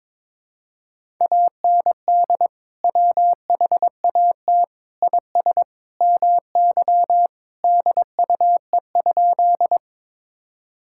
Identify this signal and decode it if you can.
Morse code, heard as a keyed tone, about 22 wpm, 700 Hz. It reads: AND WHAT IS MY DUE?
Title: Sentences from Top 1000 Words - Review Part 6 - 22wpm